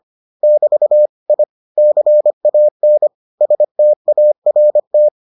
Morse code, heard as a keyed tone, about 25 wpm, 600 Hz. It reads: <BT> I CAN START